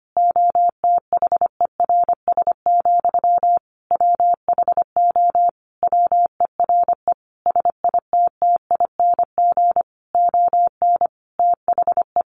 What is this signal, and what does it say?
Morse code, heard as a keyed tone, about 25 wpm, 700 Hz. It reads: OTHERS, WHO WERE SITTING ON THE
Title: A Sound Of Thunder 25wpm